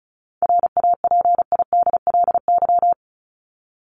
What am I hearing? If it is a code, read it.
Morse code, heard as a keyed tone, about 34 wpm, 700 Hz. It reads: RAPIDLY
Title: LY Suffix - Top 100 Words with Base Words 35wpm